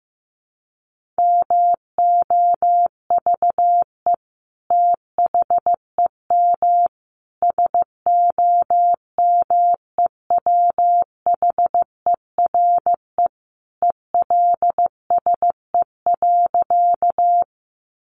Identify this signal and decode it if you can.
Morse code, heard as a keyed tone, about 15 wpm, 700 Hz. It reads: MOVE THEM SOMEWHERE ELSE.
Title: Sentences from Top 1400 Words 15wpm